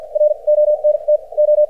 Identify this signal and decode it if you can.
Morse code, heard as a keyed tone, about 53 wpm, spaced as for 37 wpm, 590 Hz. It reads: RONTO
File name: archived / CRSnet